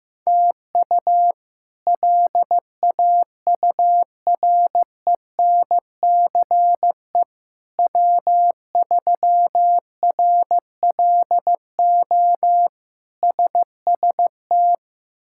Morse code, heard as a keyed tone, about 16 wpm, 700 Hz. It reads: TU LAURENCE W3RLO SST
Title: K1USN SST QSO Practice - 15wpm